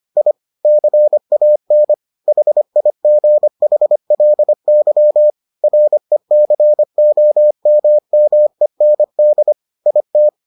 Morse code, heard as a keyed tone, about 25 wpm, 600 Hz. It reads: I CAN HIGHLY RECOMMEND IT